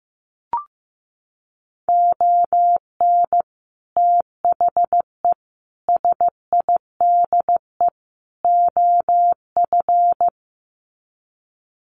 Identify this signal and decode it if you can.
Morse code, heard as a keyed tone, about 15 wpm, 700 Hz. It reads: ON THE SIDE OF
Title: Sets of 4 Words 15wpm